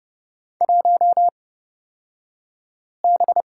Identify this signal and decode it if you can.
Morse code, heard as a keyed tone, about 30 wpm, spaced as for 5 wpm, 700 Hz. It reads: 1B
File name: Lesson 38d - Letter q - Two Characters 30wpm